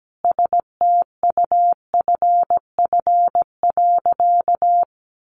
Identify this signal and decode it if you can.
Morse code, heard as a keyed tone, about 17 wpm, 700 Hz. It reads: STUFF.
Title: Sentences from Top 1400 Words 17wpm